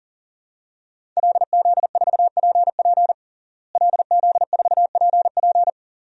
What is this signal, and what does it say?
Morse code, heard as a keyed tone, about 40 wpm, 700 Hz. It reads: LZ4PP LZ4PP
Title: DXCC-100 Call Signs - 5 Characters - Repeated 3x - 40wpm